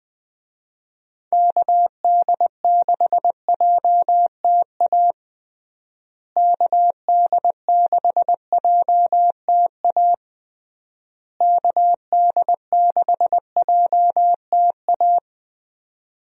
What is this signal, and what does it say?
Morse code, heard as a keyed tone, about 21 wpm, 700 Hz. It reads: KD6JTA KD6JTA KD6JTA
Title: USA Call Signs - 6 Characters - Repeated 3x - 20wpm